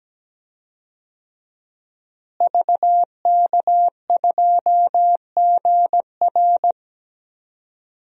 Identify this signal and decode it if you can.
Morse code, heard as a keyed tone, about 17 wpm, 700 Hz. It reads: VK2GR
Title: DXCC-100 Call Signs - 5 Characters - Repeated 3x - 17wpm